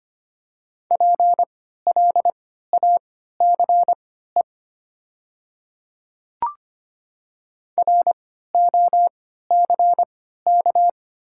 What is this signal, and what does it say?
Morse code, heard as a keyed tone, about 25 wpm, spaced as for 14 wpm, 700 Hz. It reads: PLACE ROCK
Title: 3x Character Spacing - Top 500 Words - 25wpm